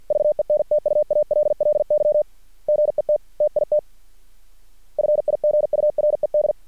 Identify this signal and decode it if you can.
Morse code, heard as a keyed tone, about 50 wpm, 600 Hz. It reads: VENTUALLY GET TIT FIGURED
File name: archived / CRSnet